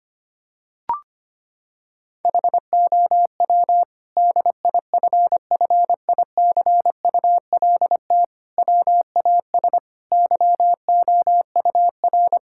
Morse code, heard as a keyed tone, about 25 wpm, 700 Hz. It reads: HOW DIFFICULT WAS YOUR
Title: Sentences from Top 1000 Words - Review Part 6 - 25wpm